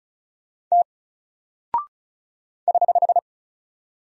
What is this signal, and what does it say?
Morse code, heard as a keyed tone, about 36 wpm, 700 Hz. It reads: T <HH>
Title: Single Character (Letter:Number:Punctuation:Prosign) 35wpm